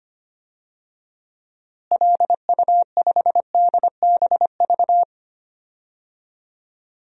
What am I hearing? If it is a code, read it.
Morse code, heard as a keyed tone, about 25 wpm, 700 Hz. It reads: LU5DBV
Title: DXCC-100 Call Signs - 6 Characters - Repeated 3x - 25wpm